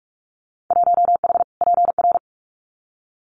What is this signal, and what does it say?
Morse code, heard as a keyed tone, about 46 wpm, 700 Hz. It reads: 1H PR
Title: Winter Field Day Exchanges Encore 45wpm